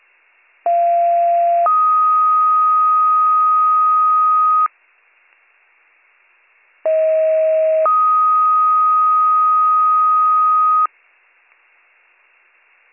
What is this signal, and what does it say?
Two-tone page, heard as a one-second tone, then a three-second tone, about 690 then 1250 Hz, about 640 then 1250 Hz. It is heard from 0.7 to 10.9 s.